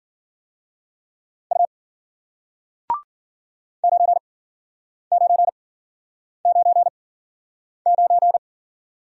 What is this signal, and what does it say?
Morse code, heard as a keyed tone, about 53 wpm, spaced as for 9 wpm, 700 Hz. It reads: U 9999